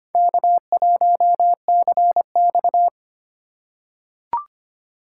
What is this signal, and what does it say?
Morse code, heard as a keyed tone, about 25 wpm, 700 Hz. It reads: K1CX
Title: K1USN SST QSO Practice - 25wpm